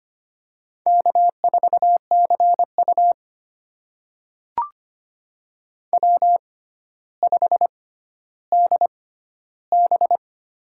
Morse code, heard as a keyed tone, about 25 wpm, spaced as for 13 wpm, 700 Hz. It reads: K4CU W5DB
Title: USA Call Signs - 6x Character Spacing - 25wpm